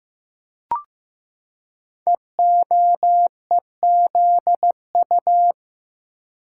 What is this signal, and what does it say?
Morse code, heard as a keyed tone, about 15 wpm, 700 Hz. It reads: EOEZU
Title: Code Groups - 5 Letters - 15wpm